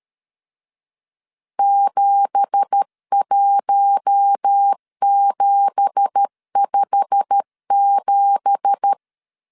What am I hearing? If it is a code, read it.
Morse code, heard as a keyed tone, about 13 wpm, 800 Hz. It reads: 71757
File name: LEZIONI / 9 2867 12wpm 2322